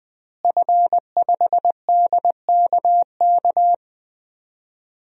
Morse code, heard as a keyed tone, about 20 wpm, 700 Hz. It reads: F5DKK
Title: DXCC-100 Call Signs - 5 Characters - Repeated 3x - 20wpm